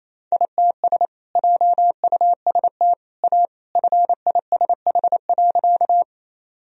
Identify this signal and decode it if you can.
Morse code, heard as a keyed tone, about 28 wpm, 700 Hz. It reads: ITS JUST A FISH.
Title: Sentences from Top 1000 Words - Review Part 1 - 28wpm